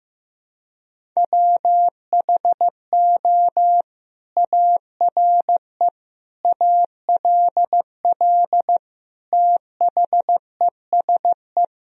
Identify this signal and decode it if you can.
Morse code, heard as a keyed tone, about 15 wpm, 700 Hz. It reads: WHO ARE ALL THESE
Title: Sentences from Top 1000 Words - Review Part 1 - 15wpm